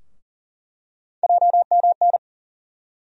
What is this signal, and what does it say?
Morse code, heard as a keyed tone, about 40 wpm, 700 Hz. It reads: JMN
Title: USA Call Sign Postfixes - 3 Letters 40wpm